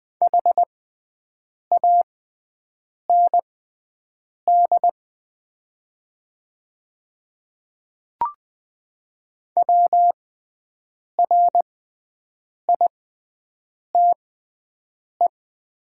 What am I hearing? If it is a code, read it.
Morse code, heard as a keyed tone, about 20 wpm, spaced as for 7 wpm, 700 Hz. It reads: HAND WRITE
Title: 6x Character Spacing - Top 300 Words - 20wpm